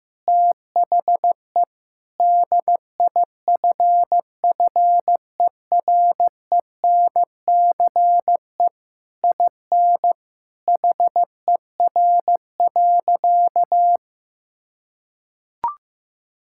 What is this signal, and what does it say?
Morse code, heard as a keyed tone, about 15 wpm, 700 Hz. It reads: THE DIFFERENCE IN HER.
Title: Sentences from Top 1000 Words - Review Part 5 - 15wpm